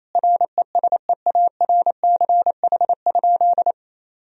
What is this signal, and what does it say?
Morse code, heard as a keyed tone, about 28 wpm, 700 Hz. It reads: RESEARCH?
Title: Sentences from Top 1000 Words - Review Part 4 - 28wpm